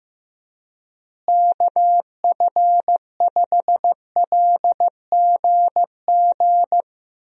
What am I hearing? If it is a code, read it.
Morse code, heard as a keyed tone, about 15 wpm, 700 Hz. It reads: KF5LGG
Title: USA Call Signs - 6 Characters - Repeated 2x - 15wpm